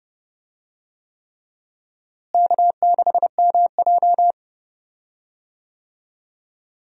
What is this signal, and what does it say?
Morse code, heard as a keyed tone, about 29 wpm, 700 Hz. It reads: K6MJ
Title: USA Call Signs - 4 Characters - Repeated 3x - 30wpm